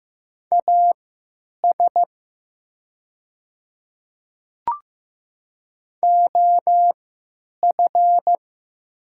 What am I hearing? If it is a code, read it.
Morse code, heard as a keyed tone, about 15 wpm, spaced as for 9 wpm, 690 Hz. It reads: AS OF
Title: 3x Character Spacing - Top 100 Words - 15wpm